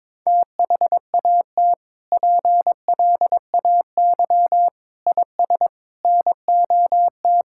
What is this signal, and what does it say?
Morse code, heard as a keyed tone, about 22 wpm, 700 Hz. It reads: THAT PLAY IS NOT
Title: Sentences from Top 1000 Words - Review Part 5 - 22wpm